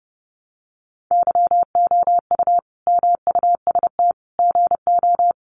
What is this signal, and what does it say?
Morse code, heard as a keyed tone, about 30 wpm, 700 Hz. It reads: YOU MUST GO
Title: Sentences from Top 1100 Words 30wpm